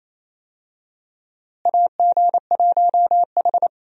Morse code, heard as a keyed tone, about 28 wpm, 700 Hz. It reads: AG1H
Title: USA Call Signs - 4 Characters - Repeated 2x - 28wpm